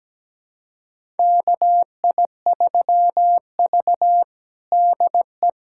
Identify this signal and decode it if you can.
Morse code, heard as a keyed tone, about 17 wpm, 700 Hz. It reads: KI3V DE
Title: USA Callsign DE Callsign 17wpm